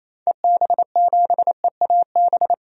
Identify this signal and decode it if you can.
Morse code, heard as a keyed tone, about 27 wpm, 700 Hz. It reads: EB7EAB
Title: DXCC-100 Call Signs - 6 Characters - 28wpm